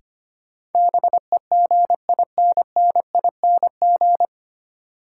Morse code, heard as a keyed tone, about 25 wpm, 700 Hz. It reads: BEGINNING
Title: Sentences from Top 800 Words - New Word Drill - 25wpm